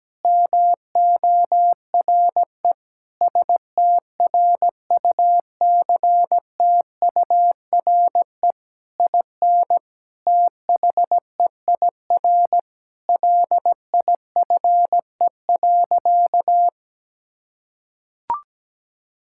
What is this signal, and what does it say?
Morse code, heard as a keyed tone, about 17 wpm, 700 Hz. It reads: MORE STRUCTURE IN THEIR LIFE.
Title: Sentences from Top 1000 Words - Review Part 6 - 17wpm